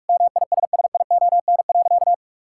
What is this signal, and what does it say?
Morse code, heard as a keyed tone, about 41 wpm, 700 Hz. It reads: MISSION.
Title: Sentences from Top 1600 Words 45wpm